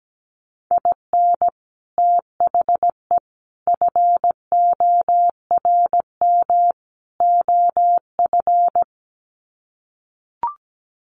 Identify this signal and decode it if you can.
Morse code, heard as a keyed tone, about 17 wpm, 700 Hz. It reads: IN THE FORM OF
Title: Sets of 4 Words 17wpm